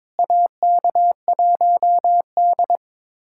Morse code, heard as a keyed tone, about 22 wpm, 700 Hz. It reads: AK1D
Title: USA Callsign DE Callsign - 4 Characters - 22wpm